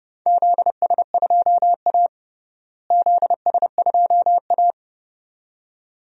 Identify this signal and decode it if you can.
Morse code, heard as a keyed tone, about 30 wpm, 700 Hz. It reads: ZS2A ZS2A
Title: DXCC-100 Call Signs - 4 Characters - Repeated 2x - 30wpm